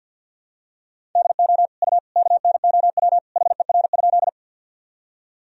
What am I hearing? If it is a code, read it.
Morse code, heard as a keyed tone, about 45 wpm, 700 Hz. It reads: DO U KNOW HER?
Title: Sentences from Top 100 Words and Abbreviations 50wpm